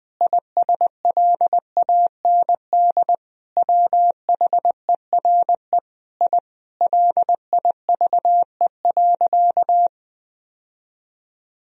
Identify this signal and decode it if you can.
Morse code, heard as a keyed tone, about 20 wpm, 700 Hz. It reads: ISLAND WHERE I LIVE.
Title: Sentences from Top 1000 Words - Review Part 3 - 20wpm